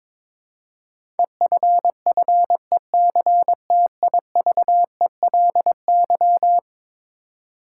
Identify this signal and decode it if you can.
Morse code, heard as a keyed tone, about 21 wpm, 700 Hz. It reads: EFFECTIVELY
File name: practice / 22wpm / Eleven Letter Words - 22wpm